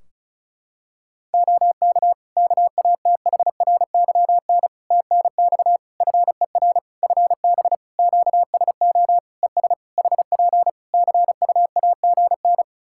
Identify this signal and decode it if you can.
Morse code, heard as a keyed tone, about 34 wpm, 700 Hz. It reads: OK KATHRYN TNX FER FB QSO ES HP CUAGN
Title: K1USN SST QSO Practice 35wpm